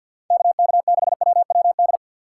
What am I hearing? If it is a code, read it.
Morse code, heard as a keyed tone, about 66 wpm, spaced as for 58 wpm, 700 Hz. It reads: KK6WWD
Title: Speed Racing - USA 2x3 Call Signs 50wpm